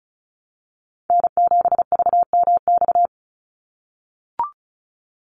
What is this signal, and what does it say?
Morse code, heard as a keyed tone, about 34 wpm, 700 Hz. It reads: N7VMX
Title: USA Call Signs 35wpm